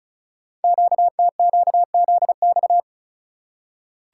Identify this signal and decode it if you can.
Morse code, heard as a keyed tone, about 36 wpm, 690 Hz. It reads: QTQZX
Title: Code Groups - 5 Letters - 35wpm